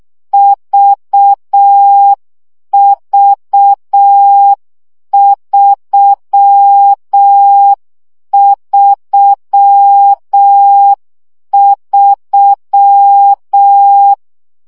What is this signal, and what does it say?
Morse code, heard as a keyed tone, about 6 wpm, 800 Hz. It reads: VV333
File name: CW gyakorlas v 3 5 baud